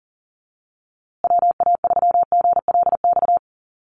Morse code, heard as a keyed tone, about 40 wpm, 700 Hz. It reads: WA3GLX
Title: USA Call Signs Repeated 2x 40wpm